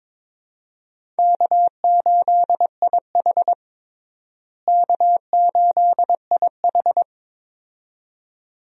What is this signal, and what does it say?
Morse code, heard as a keyed tone, about 21 wpm, 700 Hz. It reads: K8IH K8IH
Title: USA Call Signs - 4 Characters - Repeated 2x - 22wpm